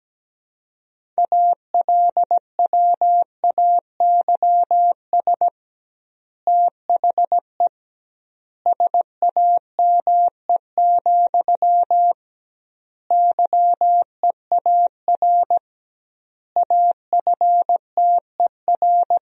Morse code, heard as a keyed tone, about 17 wpm, 700 Hz. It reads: ALWAYS THE SAME, YEAR AFTER
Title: 2x Word-Spacing - Sentences from Top 300 Words - 17wpm